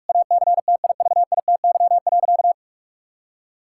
Morse code, heard as a keyed tone, about 46 wpm, 700 Hz. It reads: ACTIVITY.
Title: Sentences from Top 1000 Words - Review Part 6 - 45wpm